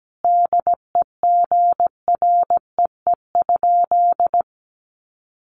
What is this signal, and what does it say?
Morse code, heard as a keyed tone, about 17 wpm, 700 Hz. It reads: DEGREE?
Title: Sentences from Top 1500 Words 17wpm